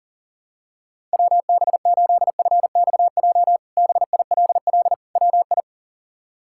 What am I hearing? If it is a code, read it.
Morse code, heard as a keyed tone, about 37 wpm, 700 Hz. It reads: WB8FXJ BILL WI
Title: K1USN SST QSO Practice - 40wpm